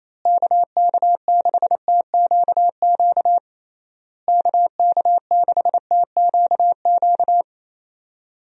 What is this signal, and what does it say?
Morse code, heard as a keyed tone, about 28 wpm, 700 Hz. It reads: KK6TQQ KK6TQQ
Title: USA Call Signs - 6 Characters - Repeated 3x - 28wpm